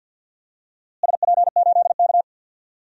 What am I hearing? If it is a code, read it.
Morse code, heard as a keyed tone, about 53 wpm, 700 Hz. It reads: SP8K
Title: DXCC-100 Call Signs - 4 Characters - Repeated 3x - 50wpm